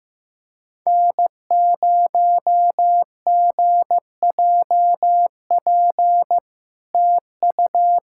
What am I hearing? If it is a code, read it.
Morse code, heard as a keyed tone, about 16 wpm, 700 Hz. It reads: N0GJP TU
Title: SOTA QSO Practice 15wpm